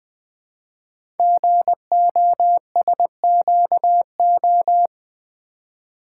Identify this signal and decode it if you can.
Morse code, heard as a keyed tone, about 20 wpm, 700 Hz. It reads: GOSQO
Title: Code Groups - 5 Letters - 20wpm